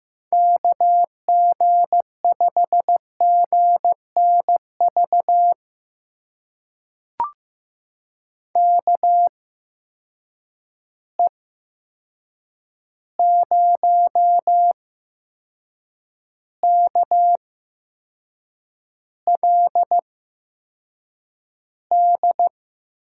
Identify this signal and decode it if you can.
Morse code, heard as a keyed tone, about 15 wpm, spaced as for 6 wpm, 700 Hz. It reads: KG5GNV KE0KLD
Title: USA Call Signs - 6 Characters - 8x Character Spacing - 15wpm